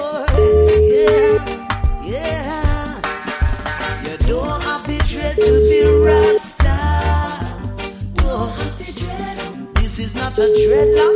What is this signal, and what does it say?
Ringback tone (Europe, 425 Hz), with noise.